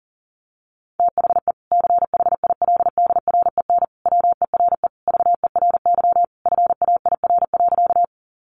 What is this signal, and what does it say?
Morse code, heard as a keyed tone, about 40 wpm, 700 Hz. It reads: THE CHILDREN WERE VERY FAIR.
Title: Sentences from Top 1300 Words 40wpm